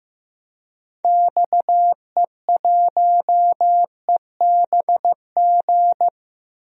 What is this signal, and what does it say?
Morse code, heard as a keyed tone, about 15 wpm, 700 Hz. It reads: XE1EBG
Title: DXCC-100 Call Signs - 6 Characters - Repeated 2x - 15wpm